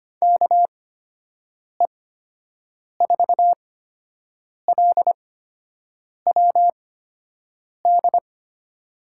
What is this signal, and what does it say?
Morse code, heard as a keyed tone, about 25 wpm, spaced as for 7 wpm, 700 Hz. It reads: KE4LWD
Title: USA Call Signs - 8x Character Spacing - 25wpm